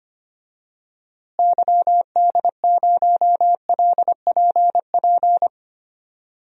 Morse code, heard as a keyed tone, about 25 wpm, 700 Hz. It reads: YD0LPP
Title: DXCC-100 Call Signs - 6 Characters - Repeated 2x - 25wpm